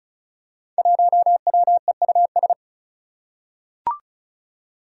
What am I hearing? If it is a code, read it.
Morse code, heard as a keyed tone, about 34 wpm, 700 Hz. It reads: 1WEUS
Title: Five Challenging Letter-Number Combinations 35wpm